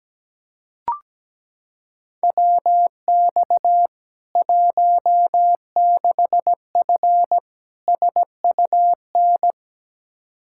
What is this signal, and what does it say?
Morse code, heard as a keyed tone, about 17 wpm, 700 Hz. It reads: WX 16F SUN
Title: K1USN SST QSO Practice 17wpm